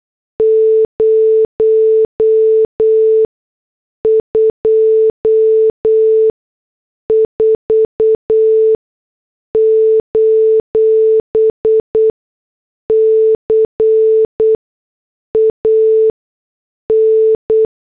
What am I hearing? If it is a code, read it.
Morse code, heard as a keyed tone, about 8 wpm, spaced as for 6 wpm, 440 Hz. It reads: 024: CAN